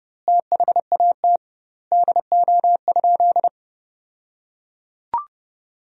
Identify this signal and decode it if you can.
Morse code, heard as a keyed tone, about 30 wpm, 700 Hz. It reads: THAT DO?
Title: 2x Word-Spacing - Sentences from Top 100 Words - 30wpm